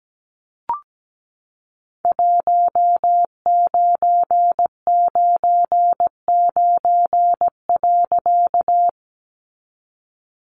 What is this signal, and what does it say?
Morse code, heard as a keyed tone, about 17 wpm, 700 Hz. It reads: 1999.
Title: A Sound Of Thunder 17wpm